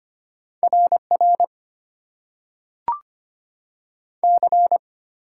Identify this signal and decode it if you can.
Morse code, heard as a keyed tone, about 25 wpm, 700 Hz. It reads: RR C
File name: Top 90 QSO Elements - 8x Character Spacing - 25wpm